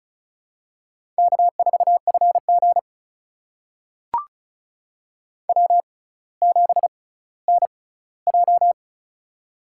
Morse code, heard as a keyed tone, about 36 wpm, spaced as for 18 wpm, 700 Hz. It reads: K4FG W7NJ
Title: USA Call Signs - 4 Characters - 6x Character Spacing - 35wpm